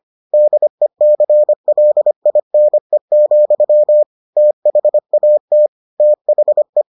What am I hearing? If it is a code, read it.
Morse code, heard as a keyed tone, about 25 wpm, 600 Hz. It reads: DECLINE, THAT THE